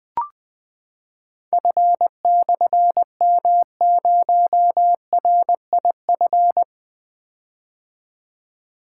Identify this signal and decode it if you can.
Morse code, heard as a keyed tone, about 20 wpm, 700 Hz. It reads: F/M0RIF
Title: Difficult Call Signs 20wpm